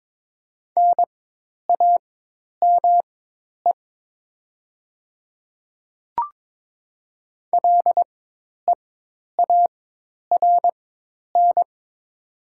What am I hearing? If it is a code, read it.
Morse code, heard as a keyed tone, about 22 wpm, spaced as for 10 wpm, 700 Hz. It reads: NAME LEARN
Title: 4x Character Spacing - Top 500 Words - 22wpm